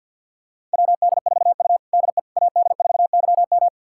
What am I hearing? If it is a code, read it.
Morse code, heard as a keyed tone, about 52 wpm, 700 Hz. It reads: WD4U DE AD4YM
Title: USA Callsign DE Callsign 50wpm